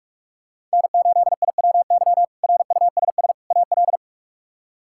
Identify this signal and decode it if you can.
Morse code, heard as a keyed tone, about 44 wpm, 700 Hz. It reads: N8IWY RUSS AL
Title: K1USN SST QSO Practice - 45wpm